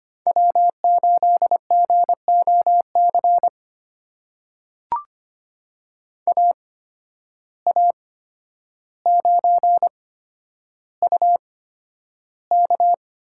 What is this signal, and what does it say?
Morse code, heard as a keyed tone, about 25 wpm, spaced as for 11 wpm, 700 Hz. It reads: W8GOC AA9UK
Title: USA Call Signs - 5 Characters - 8x Character Spacing - 25wpm